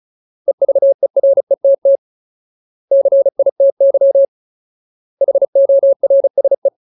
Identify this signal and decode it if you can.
Morse code, heard as a keyed tone, about 34 wpm, 550 Hz. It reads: EVERETT CITY HORSE
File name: Ultimate Word List / 35WPM Ultimate